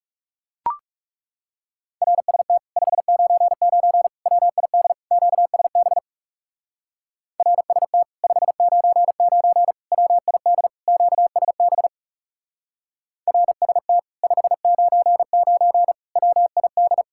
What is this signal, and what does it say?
Morse code, heard as a keyed tone, about 38 wpm, 700 Hz. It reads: RST 599 WID QSB RST 599 WID QSB RST 599 WID